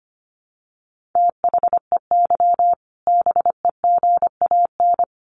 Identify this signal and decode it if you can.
Morse code, heard as a keyed tone, about 25 wpm, 700 Hz. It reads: THEY BEGAN
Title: Sentences from Top 1100 Words 25wpm